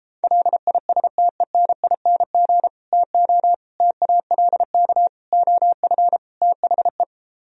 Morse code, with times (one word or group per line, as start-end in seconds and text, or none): 0.2-2.7 LISTENING
2.9-3.5 TO
3.8-5.1 TALK
5.3-6.2 OF
6.4-7.0 THE